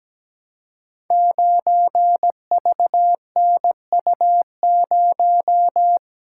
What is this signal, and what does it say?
Morse code, heard as a keyed tone, about 17 wpm, 700 Hz. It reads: 9VNU0
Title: Code Groups - 5 Letter:Numbers - 17wpm